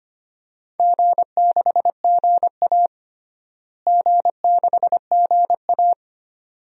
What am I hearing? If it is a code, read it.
Morse code, heard as a keyed tone, about 25 wpm, 700 Hz. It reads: G6GA G6GA
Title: DXCC-100 Call Signs - 4 Characters - Repeated 2x - 25wpm